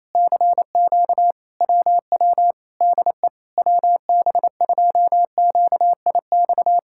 Morse code, heard as a keyed tone, about 27 wpm, 700 Hz. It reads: CQ WW DE WB2QIX
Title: Speed-Racing Common QSO Phrases Encore - 28wpm